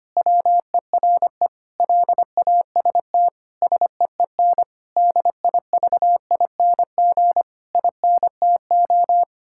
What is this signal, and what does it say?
Morse code, heard as a keyed tone, about 25 wpm, 700 Hz. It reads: WERE LAST SEEN DIVING INTO
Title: Pangram Challenge v2 - 25wpm